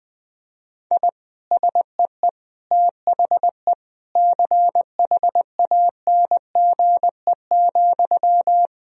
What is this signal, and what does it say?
Morse code, heard as a keyed tone, about 20 wpm, 700 Hz. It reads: I SEE THE CHANGE,
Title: Sentences from Top 1000 Words - Review Part 1 - 20wpm